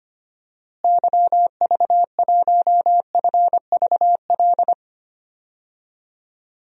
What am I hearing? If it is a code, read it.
Morse code, heard as a keyed tone, about 25 wpm, 700 Hz. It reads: YV1FVL
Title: DXCC-100 Call Signs - 6 Characters - Repeated 2x - 25wpm